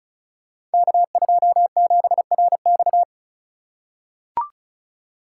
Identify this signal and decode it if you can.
Morse code, heard as a keyed tone, about 36 wpm, 700 Hz. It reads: K27RX
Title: Code Groups - 5 Letter:Numbers - 35wpm